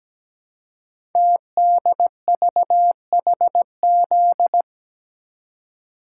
Morse code, heard as a keyed tone, about 17 wpm, 700 Hz. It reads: TDVHZ
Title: Code Groups - 5 Letters - 17wpm